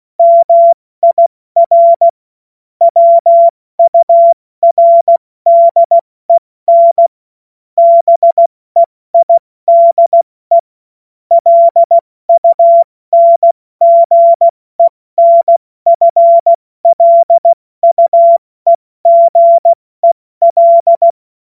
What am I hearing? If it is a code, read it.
Morse code, heard as a keyed tone, about 16 wpm, 680 Hz. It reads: MIR WURDEN BEIDE LUNGENFLUEGEL
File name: Morsekurs / Witze / Witz 13. Juli 2021, 70 F80